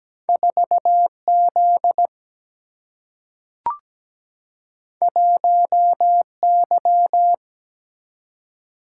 Morse code, heard as a keyed tone, about 17 wpm, 700 Hz. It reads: 4 Z 1 Y
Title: Single-Letter Single-Number 17wpm